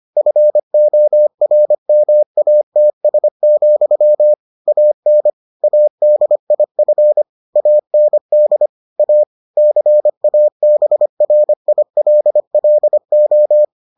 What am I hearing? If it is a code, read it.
Morse code, heard as a keyed tone, about 25 wpm, 600 Hz. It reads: FORMATS, AN ADIF AND A CABRILLO